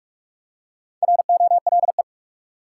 Morse code, heard as a keyed tone, about 47 wpm, 700 Hz. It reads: ROLE
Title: Four Letter Words - 45wpm